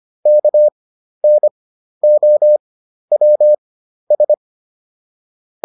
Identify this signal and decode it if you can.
Morse code, heard as a keyed tone, about 25 wpm, spaced as for 12 wpm, 600 Hz. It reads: KNOWS